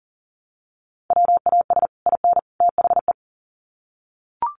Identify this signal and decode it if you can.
Morse code, heard as a keyed tone, about 40 wpm, 700 Hz. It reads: WAS IN THE
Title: Sets of 3 Words 40wpm